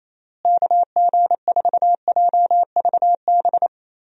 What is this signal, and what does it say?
Morse code, heard as a keyed tone, about 27 wpm, 700 Hz. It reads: KG4JVB
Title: USA Call Signs - 6 Characters - Repeated 2x - 28wpm